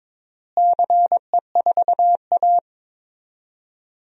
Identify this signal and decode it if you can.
Morse code, heard as a keyed tone, about 22 wpm, 700 Hz. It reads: CE4A
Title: DXCC-100 Call Signs - 4 Characters - Repeated 2x - 22wpm